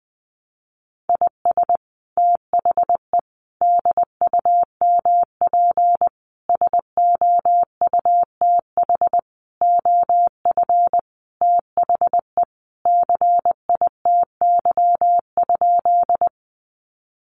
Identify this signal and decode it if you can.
Morse code, heard as a keyed tone, about 20 wpm, 700 Hz. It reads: IS THE DUMP SOUTH OF THE CITY?
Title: Sentences from Top 1300 Words 20wpm